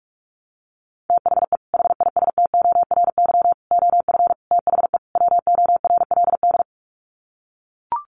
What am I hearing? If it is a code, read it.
Morse code, heard as a keyed tone, about 44 wpm, 700 Hz. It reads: THE HISTORY OF THE WORLD
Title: Sets of 5 Words Encore2 45wpm